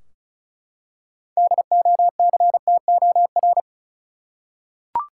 Morse code, heard as a keyed tone, about 34 wpm, 700 Hz. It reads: DOCTOR
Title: Practice QSO - Exchange 3 - More Info - 35wpm